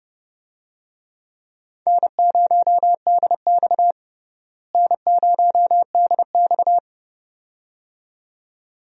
Morse code, heard as a keyed tone, about 30 wpm, 700 Hz. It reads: N0DX N0DX
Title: USA Call Signs - 4 Characters - Repeated 2x - 30wpm